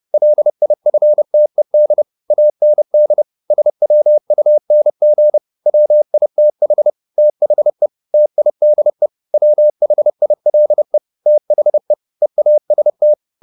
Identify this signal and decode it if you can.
Morse code, heard as a keyed tone, about 30 wpm, 600 Hz. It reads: LIFTED AND SWUNG WITH THE TIDE WHILE THE EAST